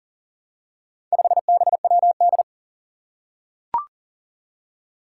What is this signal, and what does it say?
Morse code, heard as a keyed tone, about 40 wpm, 700 Hz. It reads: 5BWD
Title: Four Challenging Letter-Number Combinations 40wpm